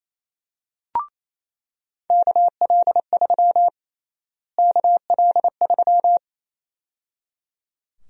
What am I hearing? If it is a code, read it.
Morse code, heard as a keyed tone, about 28 wpm, 700 Hz. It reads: KL3 KL3
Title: USA Call Sign Prefixes - Repeated 2x - 28wpm